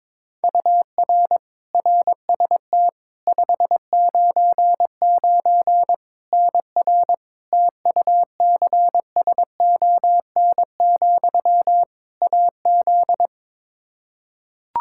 Morse code, heard as a keyed tone, about 21 wpm, 700 Hz. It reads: UR RST 599 NR TUCSON, AZ
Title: Practice QSO - Exchange 1 - Beginning - 22wpm